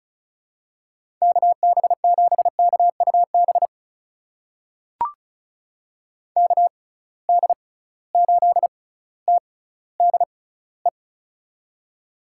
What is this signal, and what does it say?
Morse code, heard as a keyed tone, about 36 wpm, spaced as for 18 wpm, 700 Hz. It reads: KB7KUB KD8TDE
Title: USA Call Signs - 6 Characters - 6x Character Spacing - 35wpm